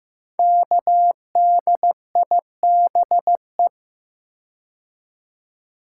Morse code, heard as a keyed tone, about 15 wpm, 700 Hz. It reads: KDIBE
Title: Code Groups - 5 Letters - 15wpm